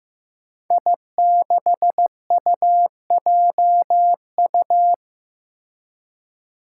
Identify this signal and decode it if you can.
Morse code, heard as a keyed tone, about 15 wpm, 700 Hz. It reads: I6UJU
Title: Five Challenging Letter-Number Combinations 15wpm